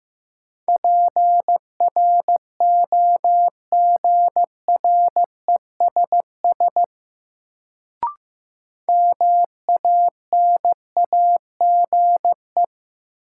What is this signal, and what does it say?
Morse code, heard as a keyed tone, about 15 wpm, 700 Hz. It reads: PROGRESS MANAGE
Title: Sentences from Top 1400 Words - New Word Drill - 15wpm